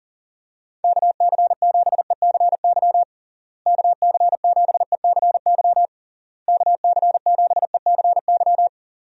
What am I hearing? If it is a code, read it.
Morse code, heard as a keyed tone, about 40 wpm, 700 Hz. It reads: KC7ECY KC7ECY KC7ECY
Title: USA Call Signs - 6 Characters - Repeated 3x - 40wpm